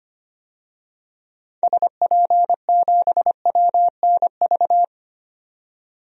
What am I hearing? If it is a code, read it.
Morse code, heard as a keyed tone, about 25 wpm, 700 Hz. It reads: SP7WNV
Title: DXCC-100 Call Signs - 6 Characters - 25wpm